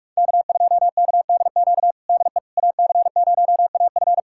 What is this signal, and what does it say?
Morse code, heard as a keyed tone, about 46 wpm, 700 Hz. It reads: K2KDQ DE AC0AF
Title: USA Callsign DE Callsign - 5 Characters - 45wpm